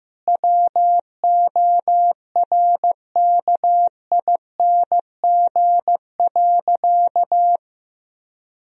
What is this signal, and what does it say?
Morse code, heard as a keyed tone, about 15 wpm, 700 Hz. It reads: WORKING.
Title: Sentences from Top 1000 Words - Review Part 4 - 15wpm